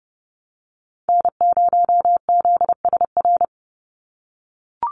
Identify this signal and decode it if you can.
Morse code, heard as a keyed tone, about 30 wpm, 700 Hz. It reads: N0ZSR
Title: USA Call Signs 30wpm